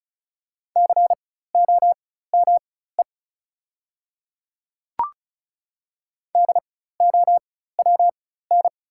Morse code, heard as a keyed tone, about 36 wpm, spaced as for 16 wpm, 690 Hz. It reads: COME DOWN
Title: 4x Character Spacing - Top 100 Words - 35wpm